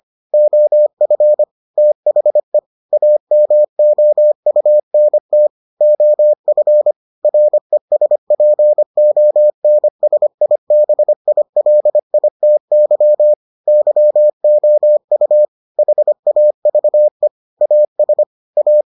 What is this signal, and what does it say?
Morse code, heard as a keyed tone, about 25 wpm, 600 Hz. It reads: OF THE AMOUNT OF RESPONSIBILITY YOU HAVE AS A